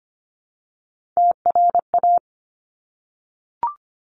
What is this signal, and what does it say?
Morse code, heard as a keyed tone, about 25 wpm, 700 Hz. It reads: TRA
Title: Top Three Letter Combinations in Words 25wpm